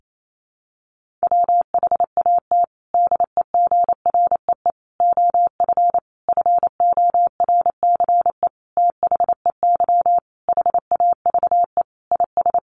Morse code, heard as a keyed tone, about 28 wpm, 700 Hz. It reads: WHAT DEGREE OF FORCE THEY HAVE IS
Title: Sentences from Top 1500 Words 28wpm